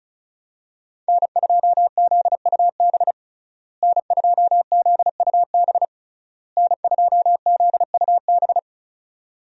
Morse code, heard as a keyed tone, about 34 wpm, 700 Hz. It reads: N2ZUB N2ZUB N2ZUB
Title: USA Call Signs - 5 Characters - Repeated 3x - 35wpm